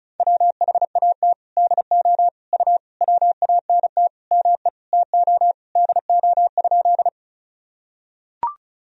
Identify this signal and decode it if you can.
Morse code, heard as a keyed tone, about 36 wpm, 700 Hz. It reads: WHAT DO U WANT ME TO DO?
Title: Sentences from Top 200 Words and Abbreviations 35wpm